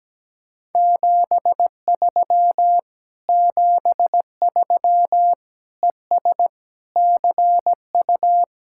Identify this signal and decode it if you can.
Morse code, heard as a keyed tone, about 17 wpm, 700 Hz. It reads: 73 73 ES CU
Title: Speed-Racing Common QSO Phrases - 17wpm